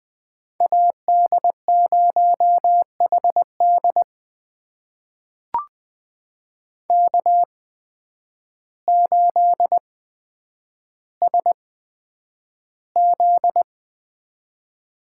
Morse code, heard as a keyed tone, about 20 wpm, spaced as for 9 wpm, 700 Hz. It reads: AD0HD K8SZ
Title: USA Call Signs - 5 Characters - 8x Character Spacing - 20wpm